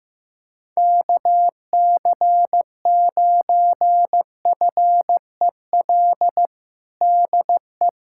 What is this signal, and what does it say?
Morse code, heard as a keyed tone, about 15 wpm, 700 Hz. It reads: KC9FEL DE
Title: USA Callsign DE Callsign - 6 Characters - 15wpm